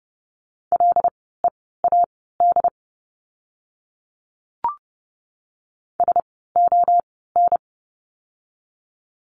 Morse code, heard as a keyed tone, about 30 wpm, spaced as for 17 wpm, 700 Hz. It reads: LEAD SON
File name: Lesson 17f - Letter u - Words 30wpm